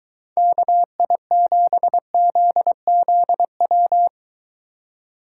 Morse code, heard as a keyed tone, about 22 wpm, 700 Hz. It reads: KI7ZZW